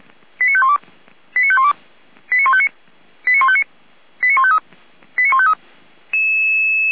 Five-tone selective call (ZVEI), heard as quick runs of five tones, tones of about 70 ms. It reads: 89512 89512 89159 89159 89153 89153